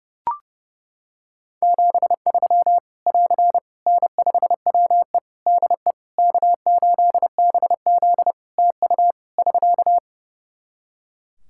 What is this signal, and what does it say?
Morse code, heard as a keyed tone, about 30 wpm, 700 Hz. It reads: 73 <AR> N5WE DE K8BZ TU <SK>
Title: K1USN SST QSO Practice 30wpm